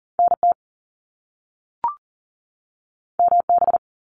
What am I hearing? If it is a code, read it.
Morse code, heard as a keyed tone, about 38 wpm, 700 Hz. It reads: NT MB
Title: Canadian Provinces and Territories 40wpm